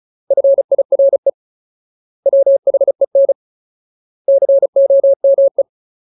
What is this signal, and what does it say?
Morse code, heard as a keyed tone, about 34 wpm, 550 Hz. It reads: FIRE WHEN COME
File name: Ultimate Word List / 35WPM Ultimate